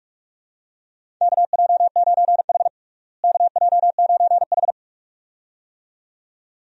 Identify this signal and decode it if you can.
Morse code, heard as a keyed tone, about 46 wpm, 700 Hz. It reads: KJ9H KJ9H
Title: USA Call Signs - 4 Characters - Repeated 2x - 45wpm